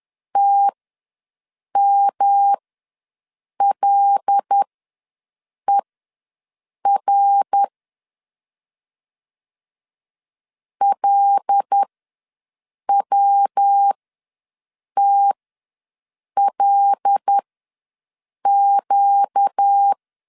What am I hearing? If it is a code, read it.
Morse code, heard as a keyed tone, about 10 wpm, spaced as for 6 wpm, 800 Hz. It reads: TMLER LWTLQ
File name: LEZIONI / 5 LBVC / 5 L ws 10wpm 3636